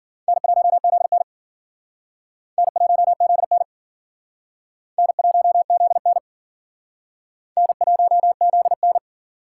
Morse code, heard as a keyed tone, about 52 wpm, 700 Hz. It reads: N1ZN N1ZN N1ZN N1ZN